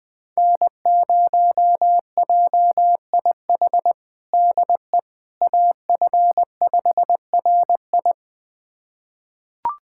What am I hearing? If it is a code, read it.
Morse code, heard as a keyed tone, about 20 wpm, 700 Hz. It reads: N0JIH DE AF5RI
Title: USA Callsign DE Callsign - 5 Characters - 20wpm